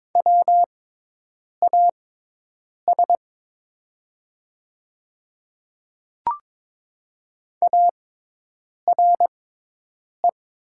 Morse code, heard as a keyed tone, about 22 wpm, spaced as for 8 wpm, 700 Hz. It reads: WAS ARE
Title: 6x Character Spacing - Top 500 Words - 22wpm